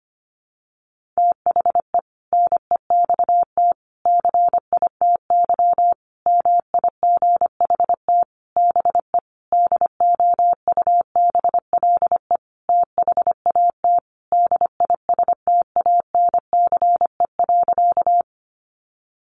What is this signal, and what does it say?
Morse code, heard as a keyed tone, about 25 wpm, 700 Hz. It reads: THE NEXT CITY MIGHT BE DOUBLE THAT DISTANCE.
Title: Sentences from Top 1400 Words 25wpm